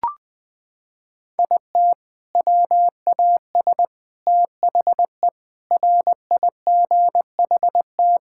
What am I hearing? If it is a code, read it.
Morse code, heard as a keyed tone, about 20 wpm, 700 Hz. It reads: IT WAS THE RIGHT
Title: Sentences with 6-Word Phrases - 20wpm